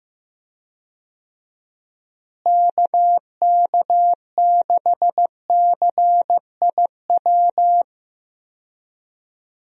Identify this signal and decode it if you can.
Morse code, heard as a keyed tone, about 15 wpm, 700 Hz. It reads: KK6CIW
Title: USA Call Signs - 6 Characters - Repeated 2x - 15wpm